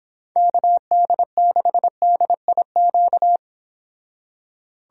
Morse code, heard as a keyed tone, about 25 wpm, 700 Hz. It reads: KD6DIQ
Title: Speed Racing - USA Call Signs 22wpm